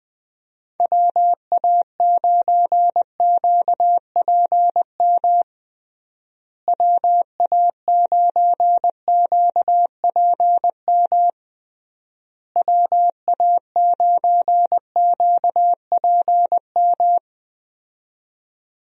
Morse code, heard as a keyed tone, about 19 wpm, 700 Hz. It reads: WA9QPM WA9QPM WA9QPM
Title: USA Call Signs - 6 Characters - Repeated 3x - 20wpm